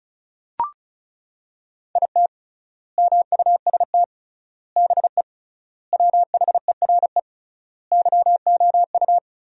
Morse code, heard as a keyed tone, about 36 wpm, 700 Hz. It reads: IT MUST BE WHERE YOU
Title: 3x Word-Spacing - Sentences from Top 200 Words - 35wpm